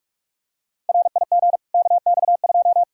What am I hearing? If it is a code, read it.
Morse code, heard as a keyed tone, about 41 wpm, 700 Hz. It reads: RIG KX2
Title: Practice QSO - Exchange 2 - Info - 45wpm